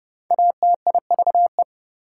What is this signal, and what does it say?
Morse code, heard as a keyed tone, about 29 wpm, 700 Hz. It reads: ATIVE
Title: Prefixes and Suffixes 30wpm